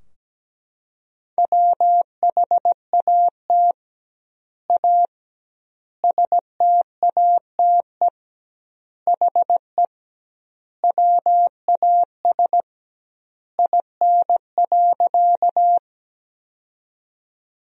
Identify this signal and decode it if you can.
Morse code, heard as a keyed tone, about 17 wpm, 700 Hz. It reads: WHAT A STATE HE WAS IN.
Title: 2x Word-Spacing - Sentences from Top 400 Words - 17wpm